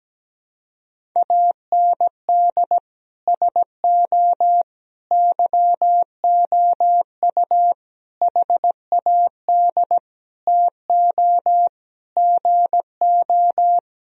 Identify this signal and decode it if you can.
Morse code, heard as a keyed tone, about 17 wpm, 700 Hz. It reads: AND SO YOU HAD TO GO
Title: Sentences from Top 1000 Words - Review Part 1 - 17wpm